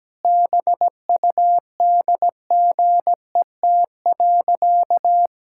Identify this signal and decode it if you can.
Morse code, heard as a keyed tone, about 17 wpm, 700 Hz. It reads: BUDGET.
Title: Pangram Challenge v2 - 17wpm